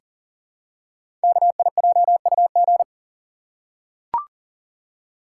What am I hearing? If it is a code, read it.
Morse code, heard as a keyed tone, about 38 wpm, 690 Hz. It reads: KIJUG
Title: Code Groups - 5 Letters - 40wpm